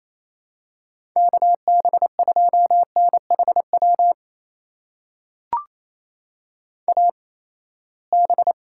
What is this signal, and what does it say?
Morse code, heard as a keyed tone, about 28 wpm, spaced as for 19 wpm, 700 Hz. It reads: KB2NHW AB
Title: USA Call Signs - 8x Character Spacing - 28wpm